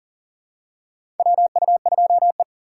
Morse code, heard as a keyed tone, about 37 wpm, 700 Hz. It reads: WU2E
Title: USA Call Signs - 4x Character Spacing - 40wpm